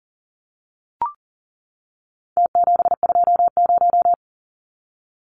Morse code, heard as a keyed tone, about 40 wpm, 700 Hz. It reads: T720
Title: Numbers - Contest Style - 4-Digit 40wpm